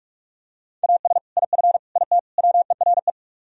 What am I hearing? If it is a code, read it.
Morse code, heard as a keyed tone, about 44 wpm, 700 Hz. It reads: AS IF IT WERE